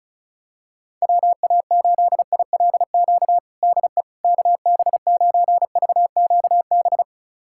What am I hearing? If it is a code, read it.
Morse code, heard as a keyed tone, about 36 wpm, 700 Hz. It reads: WA8ILQ DE KB9VQB
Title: USA Callsign DE Callsign 35wpm